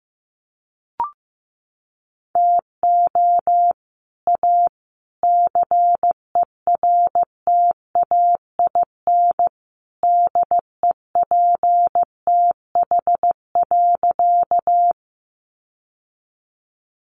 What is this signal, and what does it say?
Morse code, heard as a keyed tone, about 15 wpm, 700 Hz. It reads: TO A CERTAIN DEPTH.
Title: Sentences from Top 1200 Words 15wpm